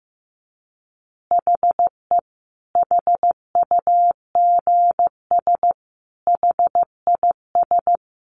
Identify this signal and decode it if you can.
Morse code, heard as a keyed tone, about 15 wpm, 700 Hz. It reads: HE HUGS HIS
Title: Easy Phrases 15wpm